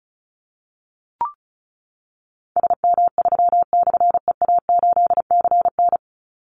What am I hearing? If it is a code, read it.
Morse code, heard as a keyed tone, about 34 wpm, 700 Hz. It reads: SM3/EA8CN
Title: Call Signs Repeated 3x - 35wpm